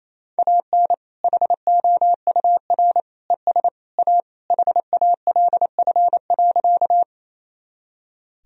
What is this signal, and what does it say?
Morse code, heard as a keyed tone, about 28 wpm, 700 Hz. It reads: AN HOUR ES A HALF.
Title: Sentences from Top 400 Words and Abbreviations 28wpm